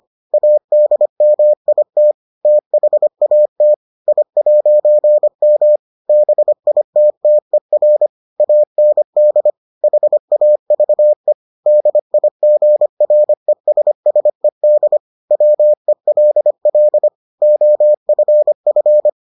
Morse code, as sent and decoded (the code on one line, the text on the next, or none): .- -.. -- .. - / - .... .- - / .. .----. -- / -... .. - - . .-. / .- -. -.. / .... .- ...- . / -.. .. --. .-. ... ... -.. / .-- . .-.. .-.. / --- ..-. ..-.
ADMIT THAT I'M BITTER AND HAVE DIGRESSED WELL OFF